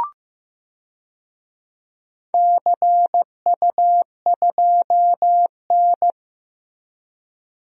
Morse code, heard as a keyed tone, about 15 wpm, 700 Hz. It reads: CU2N
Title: DXCC-100 Call Signs - 4 Characters - Repeated 2x - 15wpm